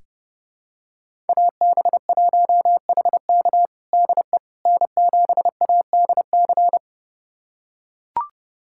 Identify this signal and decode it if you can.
Morse code, heard as a keyed tone, about 30 wpm, 700 Hz. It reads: AB1HK DE N7ADC
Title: USA Callsign DE Callsign - 5 Characters - 30wpm